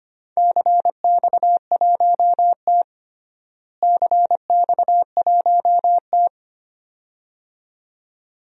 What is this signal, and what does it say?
Morse code, heard as a keyed tone, about 25 wpm, 700 Hz. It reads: CX1T CX1T
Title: DXCC-100 Call Signs - 4 Characters - Repeated 2x - 25wpm